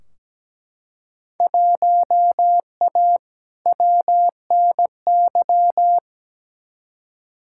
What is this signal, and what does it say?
Morse code, heard as a keyed tone, about 17 wpm, 700 Hz. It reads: 1A WNY
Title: Field Day Exchanges Encore - 17wpm